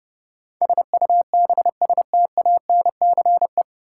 Morse code, heard as a keyed tone, about 31 wpm, 700 Hz. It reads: SUBSTANCE
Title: Sentences from Top 1500 Words - New Word Drill Alt - 30wpm